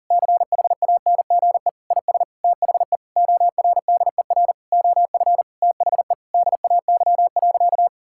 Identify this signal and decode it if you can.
Morse code, heard as a keyed tone, about 40 wpm, 700 Hz. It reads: CHANGE IS THE ORDER OF THE DAY.
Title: Sentences from Top 1000 Words - Review Part 5 - 40wpm